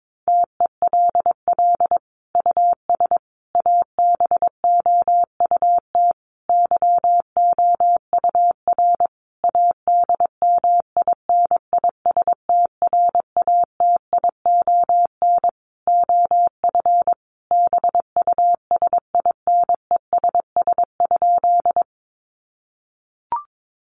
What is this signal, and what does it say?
Morse code, heard as a keyed tone, about 22 wpm, 700 Hz. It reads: TELL US ABOUT YOUR ADMINISTRATION OF BUSINESS?
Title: Sentences from Top 1300 Words 22wpm